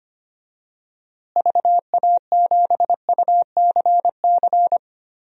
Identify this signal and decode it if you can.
Morse code, heard as a keyed tone, about 25 wpm, 700 Hz. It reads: VA7UCC
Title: DXCC-100 Call Signs - 6 Characters - Repeated 2x - 25wpm